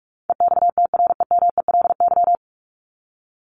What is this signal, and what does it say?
Morse code, heard as a keyed tone, about 44 wpm, 700 Hz. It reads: EXTREMELY
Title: LY Suffix - Top 100 Words with Base Words 45wpm